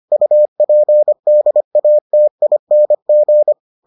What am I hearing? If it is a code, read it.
Morse code, heard as a keyed tone, about 25 wpm, 600 Hz. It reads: UPDATING